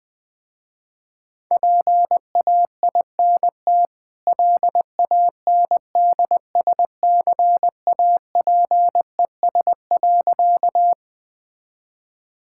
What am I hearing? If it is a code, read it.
Morse code, heard as a keyed tone, about 21 wpm, 700 Hz. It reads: PAINT LANDSCAPES.
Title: Easy Sentences with Hints - 20wpm